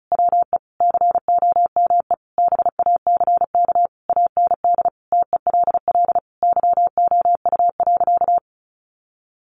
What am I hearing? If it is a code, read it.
Morse code, heard as a keyed tone, about 34 wpm, 700 Hz. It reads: WE COME BACK AND TELL YOU.
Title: Sentences from Top 1100 Words 35wpm